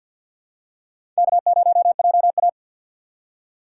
Morse code, heard as a keyed tone, about 51 wpm, 700 Hz. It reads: K0JA
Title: USA Call Signs - 4 Characters - 6x Character Spacing - 50wpm